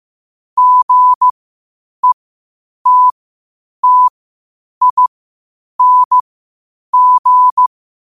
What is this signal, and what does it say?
Morse code, heard as a keyed tone, about 15 wpm, spaced as for 8 wpm, 1000 Hz. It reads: GETTING
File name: qotdfiles / 8wpm